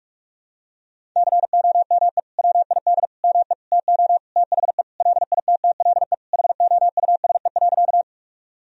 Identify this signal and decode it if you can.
Morse code, heard as a keyed tone, about 46 wpm, 700 Hz. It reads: COME WID ME TO THE LITTLE HOUSE.
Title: Sentences from Top 200 Words and Abbreviations 45wpm